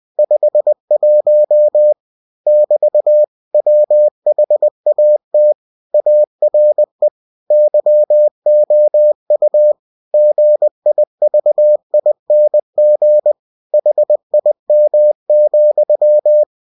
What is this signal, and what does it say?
Morse code, heard as a keyed tone, about 20 wpm, 600 Hz. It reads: 51 <BT> WHAT ARE YOU GIVING HIM,
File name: cw-books / winnie-the-pooh / winnie-20wpm-page-0051